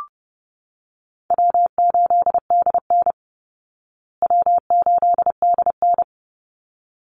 Morse code, heard as a keyed tone, about 30 wpm, 700 Hz. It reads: W8DN W8DN
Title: Easy Call Signs Repeated 2x - 30wpm